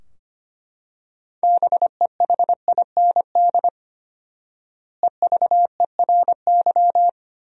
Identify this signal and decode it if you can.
Morse code, heard as a keyed tone, about 25 wpm, 700 Hz. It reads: BEHIND EVERY
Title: 4x Word-Spacing - Sentences from Top 500 Words - 25wpm